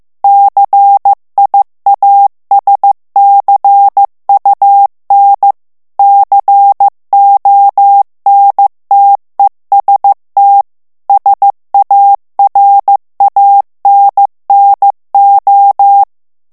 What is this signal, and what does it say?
Morse code, heard as a keyed tone, about 15 wpm, 800 Hz. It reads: CIASCUN CONTEST SARANNO
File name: morse mp3 / ITALIEN texte clair 2 vitesse 14 mots minute